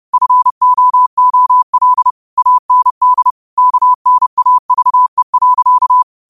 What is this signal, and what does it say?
Morse code, heard as a keyed tone, about 30 wpm, 1000 Hz. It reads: FOOL AND KNAVE.
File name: qotdfiles / 30wpm